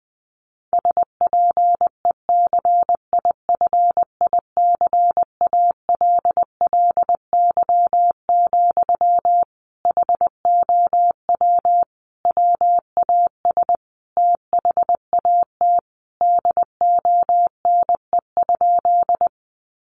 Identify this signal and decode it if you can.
Morse code, heard as a keyed tone, about 19 wpm, 700 Hz. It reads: SPECIFICALLY, HOW WAS THAT DONE?
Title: Sentences from Top 1100 Words 20wpm